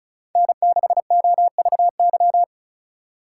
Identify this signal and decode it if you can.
Morse code, heard as a keyed tone, about 36 wpm, 700 Hz. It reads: N6OVY
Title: USA Call Signs - 5 Characters - 6x Character Spacing - 35wpm